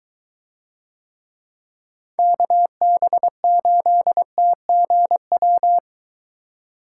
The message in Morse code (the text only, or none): KB8TGW